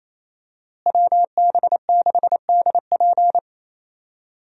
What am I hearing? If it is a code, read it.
Morse code, heard as a keyed tone, about 28 wpm, 700 Hz. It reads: WB6DP
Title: USA Call Signs - 5 Characters - Repeated 3x - 28wpm